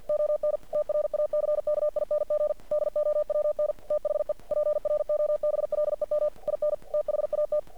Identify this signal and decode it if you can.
Morse code, heard as a keyed tone, about 50 wpm, 600 Hz. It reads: ON TRACKING DOWN THE PROBLEM IN THAT
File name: archived / CRSnet